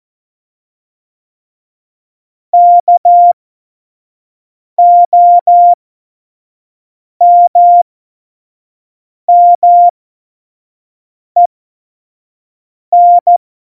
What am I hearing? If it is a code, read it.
Morse code, heard as a keyed tone, about 14 wpm, spaced as for 5 wpm, 700 Hz. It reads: KOMMEN
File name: Morsekurs / Witze / Witz 01. September 2020, 25 F70